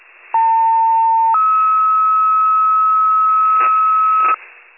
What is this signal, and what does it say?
Two-tone page, heard as a one-second tone, then a three-second tone, about 900 then 1290 Hz, with noise. It is heard from 0.3 to 4.3 s.
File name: pager / 73 RESCUE / wcalert